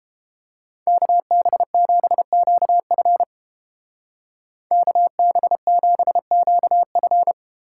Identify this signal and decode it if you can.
Morse code, heard as a keyed tone, about 32 wpm, 700 Hz. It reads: KB7QF KB7QF